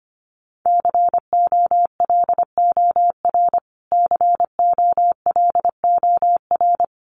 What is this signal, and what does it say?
Morse code, heard as a keyed tone, about 25 wpm, 700 Hz. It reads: COLOR COLOR
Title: Top 500 Words Repeated 25wpm